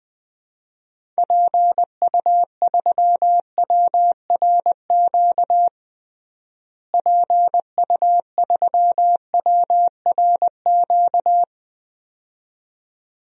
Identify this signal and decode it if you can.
Morse code, heard as a keyed tone, about 20 wpm, 700 Hz. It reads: PU3WRQ PU3WRQ
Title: DXCC-100 Call Signs - 6 Characters - Repeated 2x - 20wpm